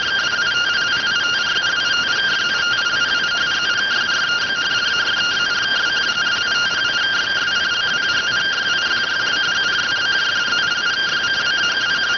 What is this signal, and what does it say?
Two-tone teleprinter signal, about 170 Hz shift.